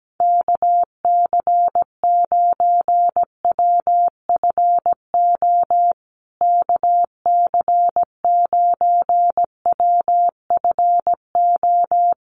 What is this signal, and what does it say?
Morse code, heard as a keyed tone, about 17 wpm, 700 Hz. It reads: KC9WFO KC9WFO
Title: USA Call Signs Repeated 3x 17wpm